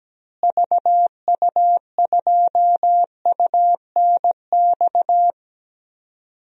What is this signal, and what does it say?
Morse code, heard as a keyed tone, about 17 wpm, 700 Hz. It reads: VU2UNX
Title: DXCC-100 Call Signs - 6 Characters - Repeated 3x - 17wpm